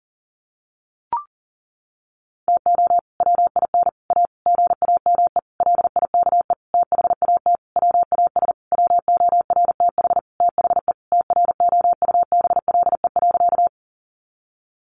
Morse code, heard as a keyed tone, about 40 wpm, 700 Hz. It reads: TO WIN A GAME LIKE THAT WAS WORTH THE TROUBLE.
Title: Sentences from Top 1500 Words 40wpm